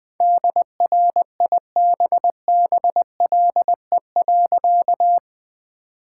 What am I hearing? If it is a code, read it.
Morse code, heard as a keyed tone, about 21 wpm, 700 Hz. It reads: DRIBBLE.
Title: All-Age Appropriate Jokes - 20wpm